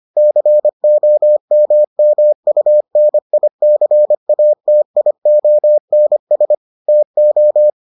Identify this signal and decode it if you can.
Morse code, heard as a keyed tone, about 25 wpm, 600 Hz. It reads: COMMUNICATIONS TO